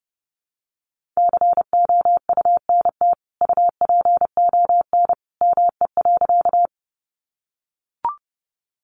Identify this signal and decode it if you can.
Morse code, heard as a keyed tone, about 28 wpm, 700 Hz. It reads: COUNT UPON ME.
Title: Sentences from Top 1500 Words 30wpm